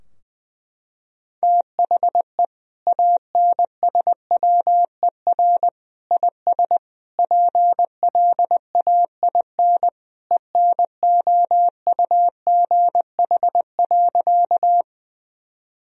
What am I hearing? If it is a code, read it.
Morse code, heard as a keyed tone, about 20 wpm, 700 Hz. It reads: THE ANSWER IS PLAIN ENOUGH.
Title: Sentences from Top 1000 Words - Review Part 2 - 20wpm